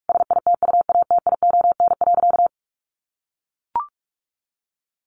Morse code, heard as a keyed tone, about 44 wpm, 700 Hz. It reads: SITUATION.
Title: Sentences from Top 1200 Words 45wpm